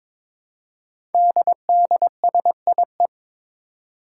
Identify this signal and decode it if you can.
Morse code, heard as a keyed tone, about 22 wpm, 700 Hz. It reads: DDSIE
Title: Five Challenging Letter-Number Combinations 22wpm